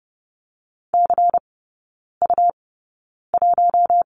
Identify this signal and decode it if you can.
Morse code, heard as a keyed tone, about 30 wpm, spaced as for 9 wpm, 700 Hz. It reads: CU1
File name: Lesson 17e - Letter u - Three Characters 30wpm